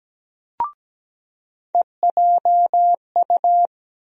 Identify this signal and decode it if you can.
Morse code, heard as a keyed tone, about 17 wpm, 700 Hz. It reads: EJU
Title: USA Call Sign Postfixes - 3 Letters 17wpm